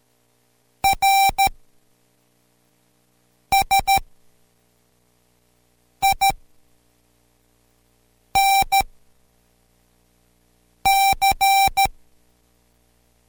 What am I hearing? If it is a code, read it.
Morse code, heard as a keyed tone, about 13 wpm, 780 Hz. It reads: R S I N C